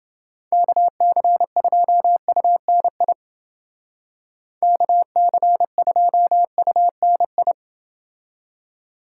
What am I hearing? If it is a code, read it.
Morse code, heard as a keyed tone, about 29 wpm, 700 Hz. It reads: KC2UNI KC2UNI